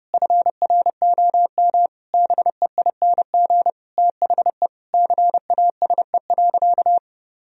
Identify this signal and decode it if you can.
Morse code, heard as a keyed tone, about 30 wpm, 700 Hz. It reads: FROM BEING THE CASE.
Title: Sentences from Top 1000 Words - Review Part 3 - 30wpm